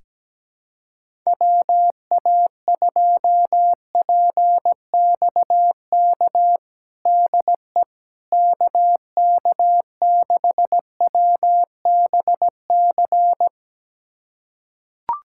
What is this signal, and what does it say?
Morse code, heard as a keyed tone, about 17 wpm, 700 Hz. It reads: WA2PXK DE KK6WBC
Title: USA Callsign DE Callsign - 6 Characters - 17wpm